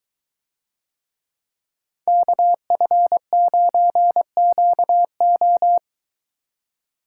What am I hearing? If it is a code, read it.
Morse code, heard as a keyed tone, about 22 wpm, 700 Hz. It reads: KF9QO